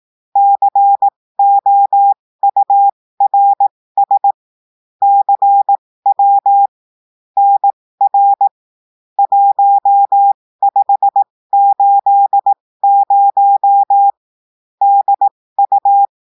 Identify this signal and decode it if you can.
Morse code, heard as a keyed tone, about 18 wpm, spaced as for 15 wpm, 800 Hz. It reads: COURS CW NR 1580 DU
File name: cours-du-lundi files / cours-1580-texte-chiffre-15-mpm